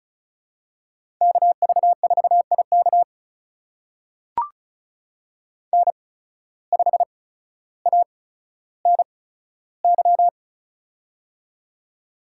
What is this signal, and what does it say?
Morse code, heard as a keyed tone, about 35 wpm, spaced as for 15 wpm, 700 Hz. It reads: KV4IK N5ANY
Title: USA Call Signs - 5 Characters - 8x Character Spacing - 35wpm